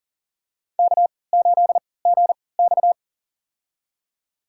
Morse code, heard as a keyed tone, about 40 wpm, spaced as for 23 wpm, 700 Hz. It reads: K8GX
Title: USA Call Signs - 4 Characters - 3x Character Spacing - 40wpm